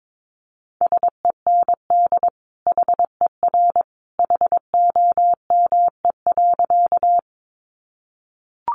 Morse code, heard as a keyed tone, about 22 wpm, 700 Hz. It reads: SEND HER HOME.
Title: Sentences from Top 1500 Words 22wpm